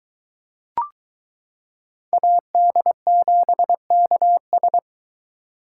Morse code, heard as a keyed tone, about 23 wpm, 700 Hz. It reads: AD7KS